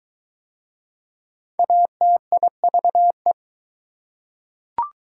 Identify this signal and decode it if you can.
Morse code, heard as a keyed tone, about 22 wpm, 700 Hz. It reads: ATIVE